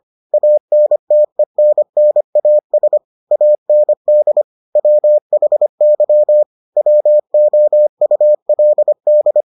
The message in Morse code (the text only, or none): ANTENNAS AND WHY WOULD